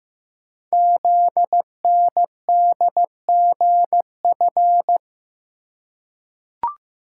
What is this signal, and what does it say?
Morse code, heard as a keyed tone, about 15 wpm, 700 Hz. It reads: ZNDGF
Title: Code Groups - 5 Letters - 15wpm